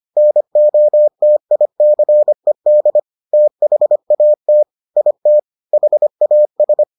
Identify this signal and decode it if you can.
Morse code, heard as a keyed tone, about 25 wpm, 600 Hz. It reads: NOTICED THAT IT HAS